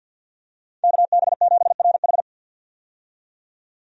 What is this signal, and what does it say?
Morse code, heard as a keyed tone, about 52 wpm, 700 Hz. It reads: KB7RH
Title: USA Call Signs - 5 Characters - 50wpm